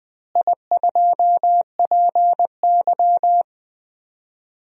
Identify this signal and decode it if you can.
Morse code, heard as a keyed tone, about 20 wpm, 700 Hz. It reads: I2PY
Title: DXCC-100 Call Signs - 4 Characters - Repeated 3x - 20wpm